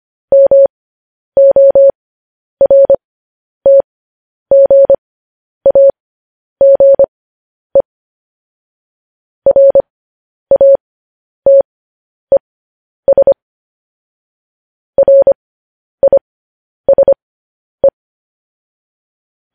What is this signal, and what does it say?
Morse code, heard as a keyed tone, about 25 wpm, spaced as for 10 wpm, 550 Hz. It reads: MORTGAGE RATES RISE
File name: podcast / News headlines-1x-25x10